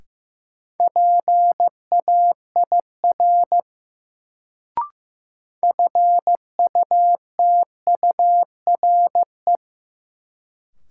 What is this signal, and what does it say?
Morse code, heard as a keyed tone, about 15 wpm, 700 Hz. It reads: PAIR FUTURE
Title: Sets of 3 Words - New Word Drill - 15wpm